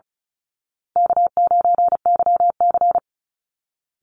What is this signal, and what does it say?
Morse code, heard as a keyed tone, about 34 wpm, 700 Hz. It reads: K9YC
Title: Easy Call Signs Repeated 3x - 35wpm